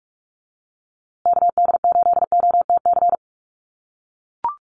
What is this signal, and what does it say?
Morse code, heard as a keyed tone, about 44 wpm, 700 Hz. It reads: KD8OTC
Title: USA Call Signs Repeated 3x 45wpm